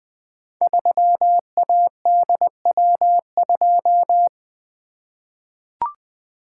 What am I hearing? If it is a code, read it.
Morse code, heard as a keyed tone, about 20 wpm, 700 Hz. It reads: 3ADW2
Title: Code Groups - 5 Letter:Numbers - 20wpm